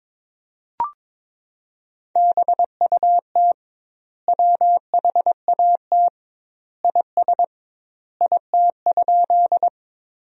Morse code, heard as a keyed tone, about 22 wpm, 700 Hz. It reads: BUT WHAT IS IT?
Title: 2x Word-Spacing - Sentences from Top 100 Words - 22wpm